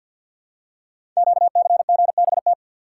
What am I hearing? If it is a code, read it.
Morse code, heard as a keyed tone, about 48 wpm, 700 Hz. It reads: QCGBT
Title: Code Groups - 5 Letter:Numbers - 50wpm